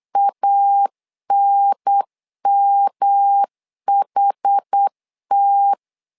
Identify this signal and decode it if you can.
Morse code, heard as a keyed tone, about 8 wpm, 800 Hz. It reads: ANMHT
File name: LEZIONI / 4 Q 8wpm 2732